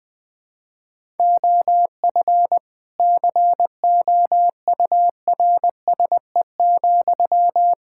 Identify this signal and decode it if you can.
Morse code, heard as a keyed tone, about 20 wpm, 700 Hz. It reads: OF COURSE,
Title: Sentences from Top 1600 Words 20wpm